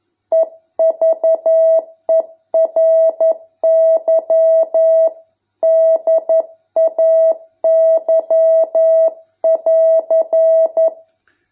Morse code, heard as a keyed tone, about 11 wpm, 650 Hz. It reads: EVERY DAY<AR>